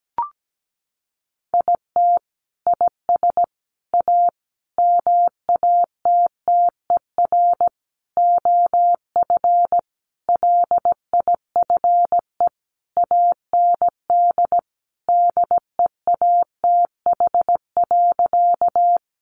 Morse code, heard as a keyed tone, about 17 wpm, 700 Hz. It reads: IT IS A MATTER OF LIFE AND DEATH.
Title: Sentences from Top 1100 Words 17wpm